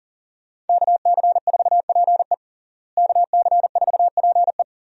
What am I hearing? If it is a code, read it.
Morse code, heard as a keyed tone, about 40 wpm, 700 Hz. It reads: KC4PE KC4PE
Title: USA Call Signs - 5 Characters - Repeated 2x - 40wpm